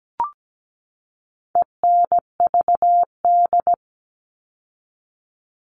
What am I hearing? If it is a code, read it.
Morse code, heard as a keyed tone, about 17 wpm, 700 Hz. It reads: ENVD
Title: All Cut Numbers - 4-Digit 17wpm